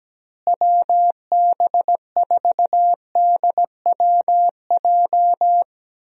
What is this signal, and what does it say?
Morse code, heard as a keyed tone, about 17 wpm, 700 Hz. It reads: WB4DWJ
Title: USA Call Signs - 6 Characters - Repeated 2x - 17wpm